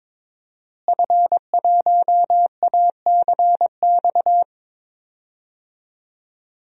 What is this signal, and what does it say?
Morse code, heard as a keyed tone, about 22 wpm, 700 Hz. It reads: F1ACX
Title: DXCC-100 Call Signs - 5 Characters - Repeated 3x - 22wpm